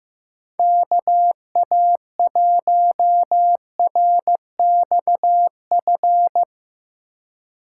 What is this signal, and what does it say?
Morse code, heard as a keyed tone, about 15 wpm, 700 Hz. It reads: KA1RXF
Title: USA Call Signs - 6 Characters - 4x Character Spacing - 15wpm